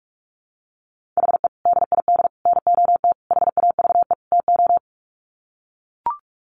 Morse code, heard as a keyed tone, about 44 wpm, 700 Hz. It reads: HE DID NOT HAVE TO
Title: Sets of 5 Words 45wpm